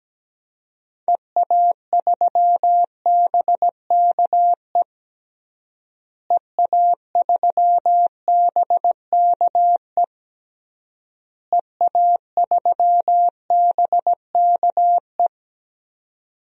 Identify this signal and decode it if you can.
Morse code, heard as a keyed tone, about 17 wpm, 700 Hz. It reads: EA3BKE EA3BKE EA3BKE
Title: DXCC-100 Call Signs - 6 Characters - Repeated 3x - 17wpm